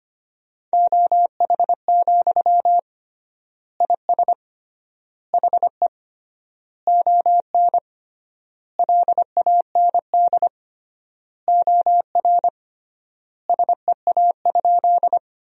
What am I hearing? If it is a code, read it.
Morse code, heard as a keyed tone, about 25 wpm, 700 Hz. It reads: OH, IS HE ON LAND OR SEA?
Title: 3x Word-Spacing - Sentences from Top 500 Words - 25wpm